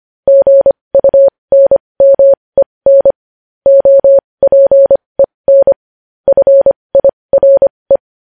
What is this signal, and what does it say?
Morse code, heard as a keyed tone, about 25 wpm, spaced as for 20 wpm, 550 Hz. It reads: GUNMEN OPEN FIRE
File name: podcast / News headlines-1x-25x20